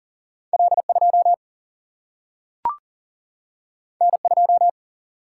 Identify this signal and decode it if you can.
Morse code, heard as a keyed tone, about 40 wpm, 700 Hz. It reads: L 2 N 2
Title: Single-Letter Single-Number 40wpm